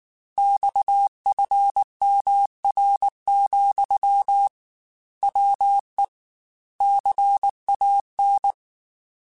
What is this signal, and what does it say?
Morse code, heard as a keyed tone, about 19 wpm, 800 Hz. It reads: XFMR, WE CAN
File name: Technical Subjects / Technical Subjects-mp3s / Sw Pwr Sup @18